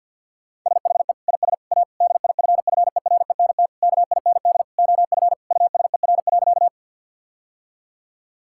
Morse code, heard as a keyed tone, about 50 wpm, 700 Hz. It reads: SHE IS A DIFFERENT KIND OF USER.
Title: Sentences from Top 1000 Words - Review Part 5 - 50wpm